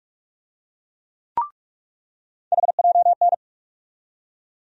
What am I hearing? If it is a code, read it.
Morse code, heard as a keyed tone, about 42 wpm, 700 Hz. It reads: HJN
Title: USA Call Sign Postfixes - 3 Letters 45wpm